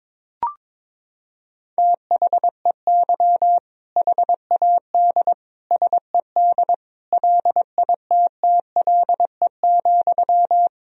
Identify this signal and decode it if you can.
Morse code, heard as a keyed tone, about 22 wpm, 700 Hz. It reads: THEY HAD SED LITTLE,
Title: Sentences from Top 400 Words and Abbreviations 22wpm